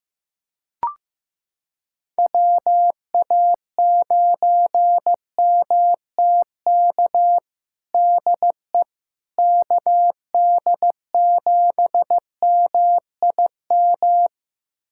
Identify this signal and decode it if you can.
Morse code, heard as a keyed tone, about 15 wpm, 700 Hz. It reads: WA9MTK DE KD7MIM
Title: USA Callsign DE Callsign - 6 Characters - 15wpm